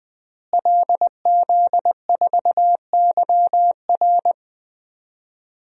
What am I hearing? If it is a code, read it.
Morse code, heard as a keyed tone, about 20 wpm, 700 Hz. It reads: LZ4YR
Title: DXCC-100 Call Signs - 5 Characters - Repeated 3x - 20wpm